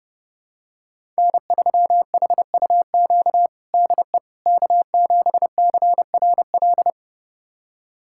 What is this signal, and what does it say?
Morse code, heard as a keyed tone, about 29 wpm, 700 Hz. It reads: N3HUQ DE K7CRL
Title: USA Callsign DE Callsign - 5 Characters - 30wpm